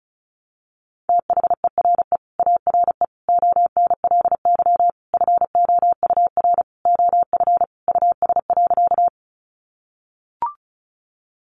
Morse code, heard as a keyed tone, about 34 wpm, 700 Hz. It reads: THERE ARE ONLY FOUR OF US.
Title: Sentences from Top 300 Words 35wpm